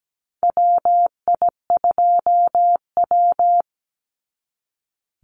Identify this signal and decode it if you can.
Morse code, heard as a keyed tone, about 17 wpm, 700 Hz. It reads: WI2W
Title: NA SOTA Chaser Callsigns 17wpm